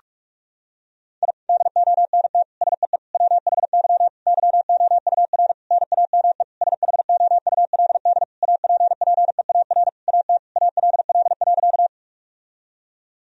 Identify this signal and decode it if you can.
Morse code, heard as a keyed tone, about 46 wpm, 700 Hz. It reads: I DONT SEE WHY YOUR NAME SHOULD APPEAR AT ALL.
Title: Sentences from Top 1000 Words - Review Part 2 - 45wpm